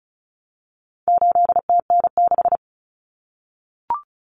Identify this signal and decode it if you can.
Morse code, heard as a keyed tone, about 34 wpm, 700 Hz. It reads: 8TN6
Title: Common Cut Numbers - 4-Digit 35wpm